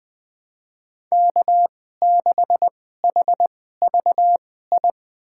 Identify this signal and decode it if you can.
Morse code, heard as a keyed tone, about 20 wpm, spaced as for 14 wpm, 700 Hz. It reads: K6HVI
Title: USA Call Signs - 5 Characters - 2x Character Spacing - 20wpm